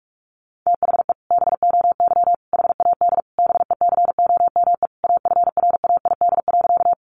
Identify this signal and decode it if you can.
Morse code, heard as a keyed tone, about 44 wpm, 700 Hz. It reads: THE BOY HAD BECOME AFRAID.
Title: Sentences from Top 1500 Words 45wpm